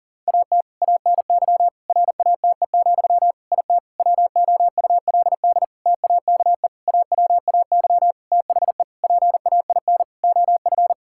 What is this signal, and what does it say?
Morse code, heard as a keyed tone, about 40 wpm, 700 Hz. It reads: AT ANY RATE, IT WOULD TAKE AWAY THE PAIN OF
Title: Sentences from Top 1600 Words 40wpm